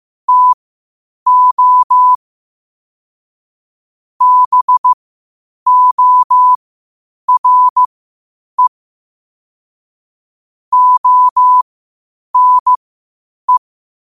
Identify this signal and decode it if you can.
Morse code, heard as a keyed tone, about 15 wpm, spaced as for 8 wpm, 1000 Hz. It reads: TO BORE ONE
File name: qotdfiles / 8wpm